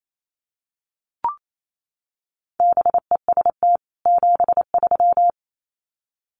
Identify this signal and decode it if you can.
Morse code, heard as a keyed tone, about 28 wpm, 700 Hz. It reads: BEST 73
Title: Common QSO Phrases 28wpm